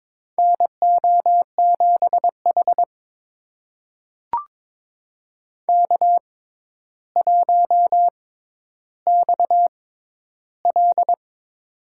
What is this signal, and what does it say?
Morse code, heard as a keyed tone, about 22 wpm, spaced as for 11 wpm, 700 Hz. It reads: NO7H K1XL
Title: USA Call Signs - 4 Characters - 6x Character Spacing - 22wpm